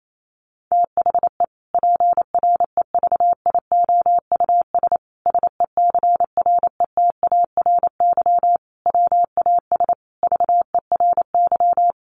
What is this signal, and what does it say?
Morse code, heard as a keyed tone, about 28 wpm, 700 Hz. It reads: THE PREVIOUS SECRETARY WAS VERY
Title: Sentences from Top 1500 Words 28wpm